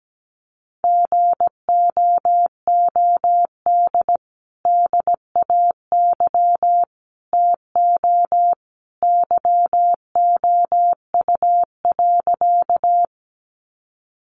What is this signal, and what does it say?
Morse code, heard as a keyed tone, about 17 wpm, 700 Hz. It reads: GOOD DAY TO YOU.
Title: Sentences from Top 200 Words 17wpm